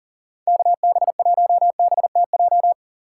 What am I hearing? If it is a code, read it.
Morse code, heard as a keyed tone, about 40 wpm, 700 Hz. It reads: KB1BTJ
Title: USA Call Signs - 6 Characters - 3x Character Spacing - 40wpm